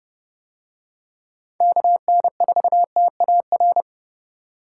Morse code, heard as a keyed tone, about 30 wpm, 700 Hz. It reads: KN4TAR